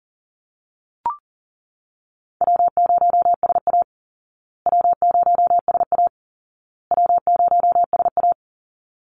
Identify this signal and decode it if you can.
Morse code, heard as a keyed tone, about 40 wpm, 700 Hz. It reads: W0SA W0SA W0SA
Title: Call Signs Repeated 2x - 40wpm